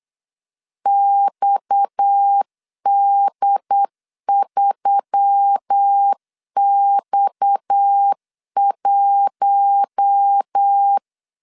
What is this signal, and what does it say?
Morse code, heard as a keyed tone, about 8 wpm, 800 Hz. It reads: XD3X1
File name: LEZIONI / 6 PKX0 / 6 PKX 8wpm 3041